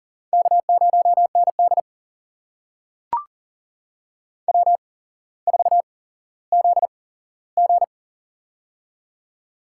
Morse code, heard as a keyed tone, about 40 wpm, spaced as for 17 wpm, 700 Hz. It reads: K0ND W4ZG
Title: USA Call Signs - 4 Characters - 8x Character Spacing - 40wpm